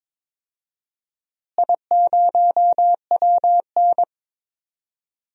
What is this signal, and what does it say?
Morse code, heard as a keyed tone, about 22 wpm, 700 Hz. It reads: I0WN
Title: DXCC-100 Call Signs - 4 Characters - Repeated 3x - 22wpm